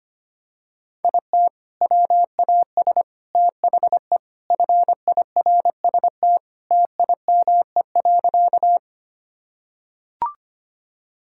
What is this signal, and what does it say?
Morse code, heard as a keyed tone, about 25 wpm, 700 Hz. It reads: IT WAS THE FIRST TIME.
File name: practice / 25wpm / Sentences with 4-Word Phrases - 25wpm